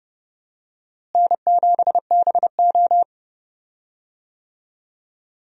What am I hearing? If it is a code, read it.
Morse code, heard as a keyed tone, about 32 wpm, 700 Hz. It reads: N7BO
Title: USA Call Signs - 4 Characters - Repeated 2x - 30wpm